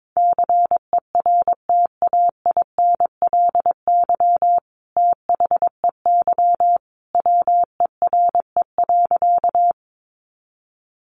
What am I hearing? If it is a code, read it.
Morse code, heard as a keyed tone, about 22 wpm, 700 Hz. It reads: CERTAINLY THEY WERE.
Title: Sentences from Top 900 Words 22wpm